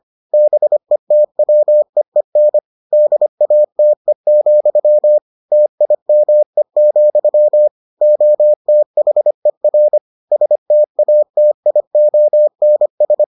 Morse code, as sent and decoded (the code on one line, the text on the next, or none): -... - .-- . -. / -.. .- - . --..-- / - .. -- . --..-- / --- - .... .-. / ... - .- - .. --- -. ...
BETWEEN DATE, TIME, OTHER STATIONS